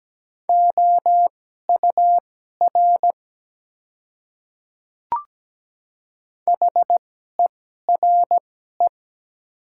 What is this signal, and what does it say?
Morse code, heard as a keyed tone, about 17 wpm, spaced as for 12 wpm, 700 Hz. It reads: OUR HERE
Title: 2x Character Spacing - Top 200 Words - 17wpm